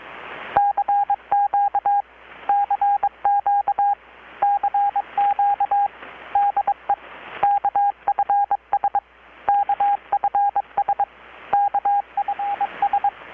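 Morse code, heard as a keyed tone, about 22 wpm, 810 Hz, with radio noise. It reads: CQ CQ CQ DE KFS KFS KFS
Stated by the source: Morse from KPH's Night of Nights XX maritime radiotelegraphy event, received in Chiapas, Mexico about 3,800 km away on a Kenwood R-600 with a grounded 10 m long-wire antenna.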